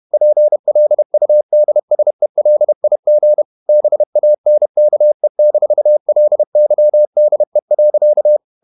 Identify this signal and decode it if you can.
Morse code, heard as a keyed tone, about 31 wpm, 600 Hz. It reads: PLUDSELIG BANKE-LYDE.
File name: Ferie QRQ 31wpm